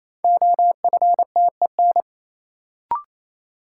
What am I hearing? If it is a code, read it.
Morse code, heard as a keyed tone, about 28 wpm, 700 Hz. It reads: OFTEN
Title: Sentences from Top 300 Words - New Word Drill - 28wpm